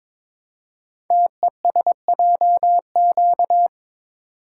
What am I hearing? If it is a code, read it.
Morse code, heard as a keyed tone, about 22 wpm, 700 Hz. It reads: TESJQ
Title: Code Groups - 5 Letters - 22wpm